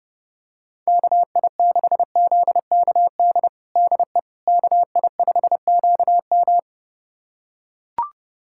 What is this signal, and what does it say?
Morse code, heard as a keyed tone, about 32 wpm, 700 Hz. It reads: KI6ZKD DE KI5QM
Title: USA Callsign DE Callsign 30wpm